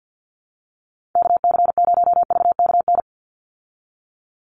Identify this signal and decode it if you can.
Morse code, heard as a keyed tone, about 50 wpm, 700 Hz. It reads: KC0VKN
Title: Call Signs Repeated 2x - 50wpm